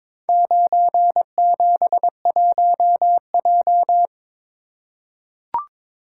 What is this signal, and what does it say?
Morse code, heard as a keyed tone, about 23 wpm, 700 Hz. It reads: 971J
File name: practice / 22wpm / Four Letter-Number Combinations - 22wpm